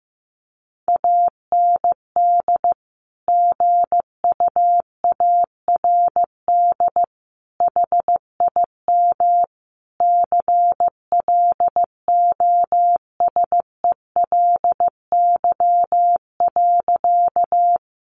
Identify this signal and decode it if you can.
Morse code, heard as a keyed tone, about 15 wpm, 700 Hz. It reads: AND GUARD HIM CLOSELY.
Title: Sentences from Top 1400 Words 15wpm